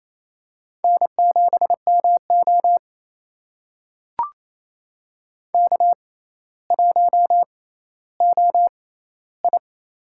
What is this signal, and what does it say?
Morse code, heard as a keyed tone, about 28 wpm, spaced as for 14 wpm, 700 Hz. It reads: N7MO K1OI
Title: USA Call Signs - 4 Characters - 6x Character Spacing - 28wpm